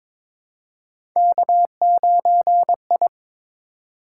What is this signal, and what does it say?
Morse code, heard as a keyed tone, about 22 wpm, 700 Hz. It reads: K9I
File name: practice / 22wpm / Three Letter-Number Combinations - 22wpm